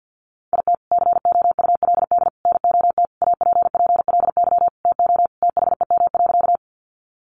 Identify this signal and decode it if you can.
Morse code, heard as a keyed tone, about 50 wpm, 700 Hz. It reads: IT COULD NOT APPLY TO THEM.
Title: Sentences from Top 1200 Words 50wpm